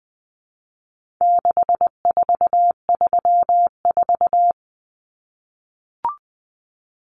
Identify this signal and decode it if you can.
Morse code, heard as a keyed tone, about 19 wpm, 700 Hz. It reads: 6434
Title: Numbers - 4-Digit 20wpm